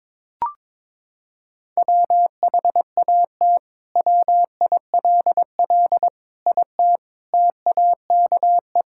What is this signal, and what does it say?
Morse code, heard as a keyed tone, about 22 wpm, 700 Hz. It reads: WHAT WILL IT TAKE
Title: Sentences from Top 1600 Words 22wpm